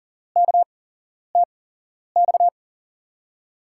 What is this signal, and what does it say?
Morse code, heard as a keyed tone, about 40 wpm, spaced as for 11 wpm, 700 Hz. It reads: KTX
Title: USA Call Signs - 5 Characters - 8x Character Spacing - 40wpm